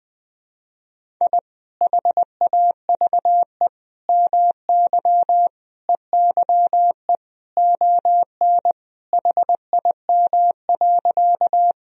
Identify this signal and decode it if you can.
Morse code, heard as a keyed tone, about 21 wpm, 700 Hz. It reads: I HAVE MY EYE ON HIM.
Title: Sentences from Top 1000 Words - Review Part 1 - 20wpm